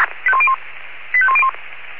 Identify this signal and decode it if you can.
Five-tone selective call (ZVEI), heard as quick runs of five tones, tones of about 70 ms. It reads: 85101